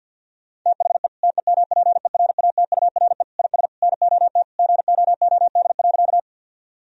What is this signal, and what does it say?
Morse code, heard as a keyed tone, about 50 wpm, 700 Hz. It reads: THE TEMPERATURE IS NOT GOOD.
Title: Sentences from Top 1000 Words - Review Part 4 - 50wpm